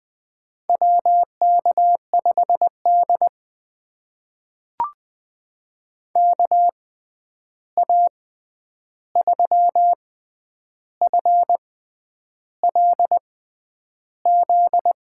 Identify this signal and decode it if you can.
Morse code, heard as a keyed tone, about 20 wpm, spaced as for 9 wpm, 700 Hz. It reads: WK5D KA3FLZ
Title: USA Call Signs - 6x Character Spacing - 20wpm